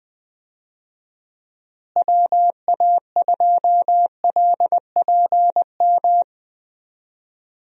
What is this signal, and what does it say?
Morse code, heard as a keyed tone, about 21 wpm, 700 Hz. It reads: WA2LPM
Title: USA Call Signs - 6 Characters - Repeated 3x - 20wpm